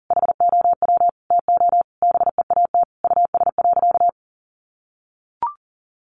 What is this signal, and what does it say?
Morse code, heard as a keyed tone, about 40 wpm, 700 Hz. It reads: HOW TO BEAT US.
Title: Sentences from Top 1300 Words 40wpm